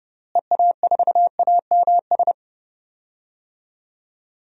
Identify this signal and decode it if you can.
Morse code, heard as a keyed tone, about 30 wpm, 700 Hz. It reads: EA4AMS
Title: DXCC-100 Call Signs - 6 Characters - Repeated 3x - 30wpm